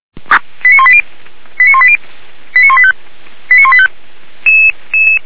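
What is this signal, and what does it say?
Five-tone selective call (ZVEI), heard as quick runs of five tones, tones of about 70 ms. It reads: 89180 89180 89185 89185
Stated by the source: ZVEI five-tone selective call received on an SDR, fire brigade alerting.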